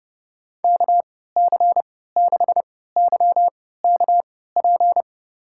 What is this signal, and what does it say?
Morse code, heard as a keyed tone, about 32 wpm, spaced as for 17 wpm, 700 Hz. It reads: KC6YKP
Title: USA Call Signs - 3x Character Spacing - 30wpm